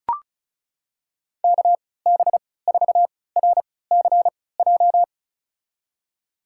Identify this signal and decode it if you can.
Morse code, heard as a keyed tone, about 36 wpm, spaced as for 20 wpm, 700 Hz. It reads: KB4RCJ
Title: USA Call Signs - 6 Characters - 3x Character Spacing - 35wpm